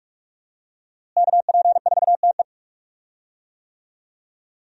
Morse code, heard as a keyed tone, about 43 wpm, 700 Hz. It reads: KP4TE
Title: DXCC-100 Call Signs - 5 Characters - Repeated 2x - 45wpm